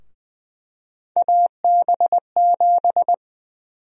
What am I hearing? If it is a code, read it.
Morse code, heard as a keyed tone, about 20 wpm, 700 Hz. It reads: AB7
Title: USA Call Sign Prefixes - 20wpm